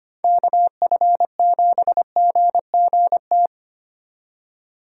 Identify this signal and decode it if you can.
Morse code, heard as a keyed tone, about 33 wpm, 700 Hz. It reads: KF7GGT